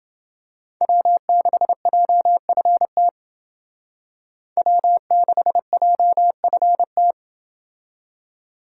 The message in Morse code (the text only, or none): W6JFT W6JFT